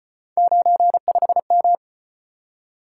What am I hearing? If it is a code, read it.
Morse code, heard as a keyed tone, about 35 wpm, 700 Hz. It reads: 95M